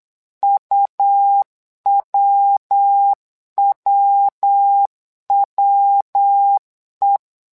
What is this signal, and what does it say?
Morse code, heard as a keyed tone, about 9 wpm, 800 Hz. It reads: UWWWE
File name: LEZIONI / 3 MWRO / 3 MWRO 8wpm 2606